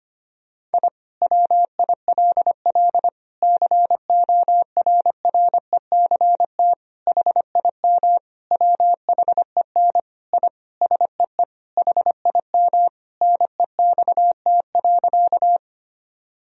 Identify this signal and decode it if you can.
Morse code, heard as a keyed tone, about 25 wpm, 700 Hz. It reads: I WILL CORRECT HIM WHEN I SEE HIM NEXT.
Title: Sentences from Top 1000 Words - Review Part 2 - 25wpm